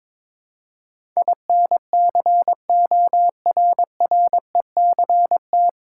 Morse code, heard as a keyed tone, about 22 wpm, 700 Hz. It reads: INCORRECT
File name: 9 Letter Words - First 3 Letters Given - 22wpm